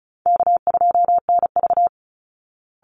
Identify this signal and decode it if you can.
Morse code, heard as a keyed tone, about 34 wpm, 700 Hz. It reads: K2NV
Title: Easy Call Signs Repeated 2x - 35wpm